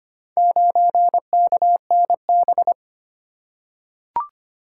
Morse code, heard as a keyed tone, about 25 wpm, 700 Hz. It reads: 9KNB
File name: practice / 25wpm / Four Letter-Number Combinations - 25wpm